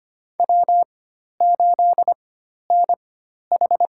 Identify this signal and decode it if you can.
Morse code, heard as a keyed tone, about 25 wpm, spaced as for 12 wpm, 700 Hz. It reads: W8NH
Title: USA Call Signs - 4 Characters - 4x Character Spacing - 25wpm